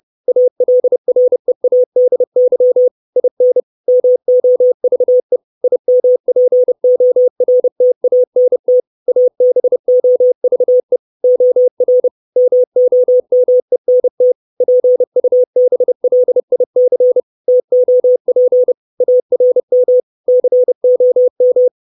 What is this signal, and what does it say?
Morse code, heard as a keyed tone, about 30 wpm, 500 Hz. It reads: ALREADY IN MOVE IMPORTANT ABOVE OR MOMENT PUBLIC TOP ARM COM
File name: 632 Words 30WPM